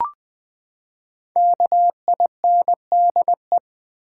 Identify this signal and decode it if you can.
Morse code, heard as a keyed tone, about 20 wpm, 700 Hz. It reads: KINDE
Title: Build A Word - 5 Letters At A Time - 12-Letter Words - 20wpm